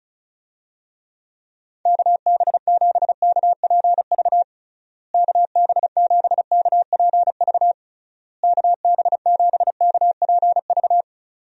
Morse code, heard as a keyed tone, about 34 wpm, 700 Hz. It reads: KB7KPV KB7KPV KB7KPV
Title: USA Call Signs - 6 Characters - Repeated 3x - 35wpm